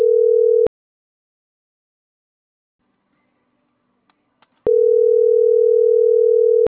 Ringback tone (North America), 440 + 480 Hz.